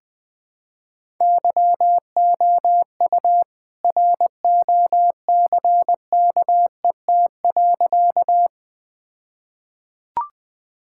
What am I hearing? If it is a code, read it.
Morse code, heard as a keyed tone, about 20 wpm, 700 Hz. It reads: YOU ROCKET.
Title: All-Age Appropriate Jokes - 20wpm